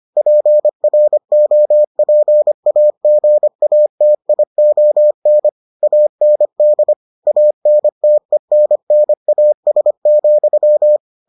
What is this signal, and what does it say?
Morse code, heard as a keyed tone, about 25 wpm, 600 Hz. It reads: PROPAGATION AND ANTENNAS,